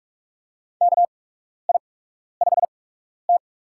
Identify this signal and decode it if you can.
Morse code, heard as a keyed tone, about 46 wpm, spaced as for 12 wpm, 700 Hz. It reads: KI5T
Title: USA Call Signs - 8x Character Spacing - 45wpm